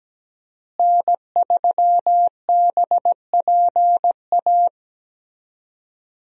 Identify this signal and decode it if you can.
Morse code, heard as a keyed tone, about 17 wpm, 700 Hz. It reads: N3BPA
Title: USA Call Signs - 5 Characters - Repeated 2x - 17wpm